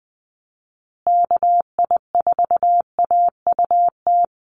Morse code, heard as a keyed tone, about 20 wpm, 700 Hz. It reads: KI4AUT
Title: USA Call Signs Repeated 2x 20wpm